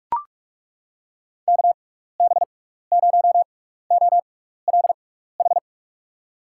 Morse code, heard as a keyed tone, about 41 wpm, spaced as for 15 wpm, 700 Hz. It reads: KB0OLH
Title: USA Call Signs - 6 Characters - 6x Character Spacing - 45wpm